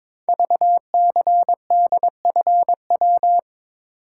Morse code, heard as a keyed tone, about 22 wpm, 700 Hz. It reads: VCDFW
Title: Code Groups - 5 Letters - 22wpm